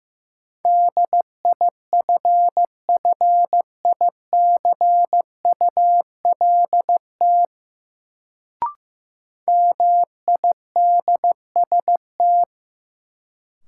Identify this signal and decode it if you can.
Morse code, heard as a keyed tone, about 15 wpm, 700 Hz. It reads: DIFFICULT MIDST
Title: Sets of 4 Words - New Word Drill - 15wpm